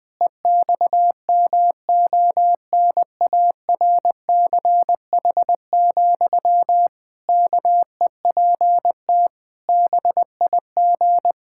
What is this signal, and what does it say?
Morse code, heard as a keyed tone, about 21 wpm, 700 Hz. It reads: EXMONARCH, KEPT BIG
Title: Pangram Challenge v2 - 20wpm